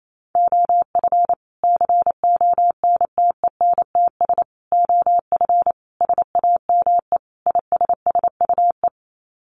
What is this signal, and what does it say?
Morse code, heard as a keyed tone, about 28 wpm, 700 Hz. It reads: OF CONTENTS OF SAME ISSUE
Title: Sets of 5 Words Encore2 28wpm